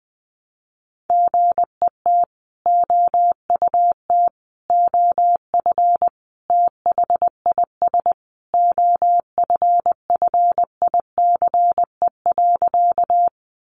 Morse code, heard as a keyed tone, about 20 wpm, 700 Hz. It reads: GET OUT OF THIS OFFICE.
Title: Sentences from Top 1000 Words 20wpm